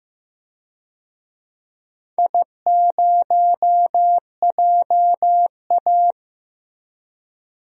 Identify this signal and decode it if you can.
Morse code, heard as a keyed tone, about 15 wpm, 700 Hz. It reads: I0JA
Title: DXCC-100 Call Signs - 4 Characters - Repeated 2x - 15wpm